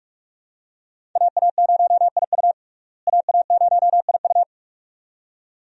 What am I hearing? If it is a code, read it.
Morse code, heard as a keyed tone, about 41 wpm, 700 Hz. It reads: AA0IU AA0IU
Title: USA Call Signs - 5 Characters - Repeated 2x - 45wpm